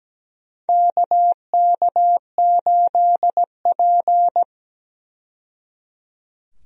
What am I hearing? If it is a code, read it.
Morse code, heard as a keyed tone, about 17 wpm, 700 Hz. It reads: KK8P
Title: USA Call Signs - 4 Characters - 17wpm